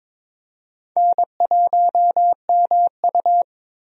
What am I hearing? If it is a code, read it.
Morse code, heard as a keyed tone, about 22 wpm, 700 Hz. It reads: N1MU
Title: USA Call Signs - 4 Characters - Repeated 2x - 22wpm